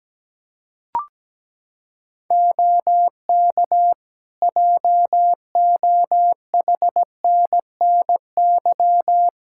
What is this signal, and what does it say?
Morse code, heard as a keyed tone, about 17 wpm, 700 Hz. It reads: OK JOHNNY
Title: Practice QSO - Exchange 3 - More Info - 17wpm